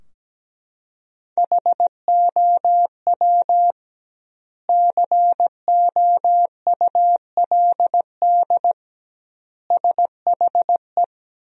Morse code, heard as a keyed tone, about 17 wpm, 700 Hz. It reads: HOW COULD SHE
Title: 2x Word-Spacing - Sentences from Top 200 Words - 17wpm